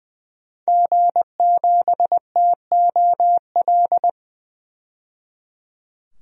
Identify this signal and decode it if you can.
Morse code, heard as a keyed tone, about 20 wpm, 700 Hz. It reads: G7TOL
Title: DXCC-100 Call Signs - 5 Characters - 20wpm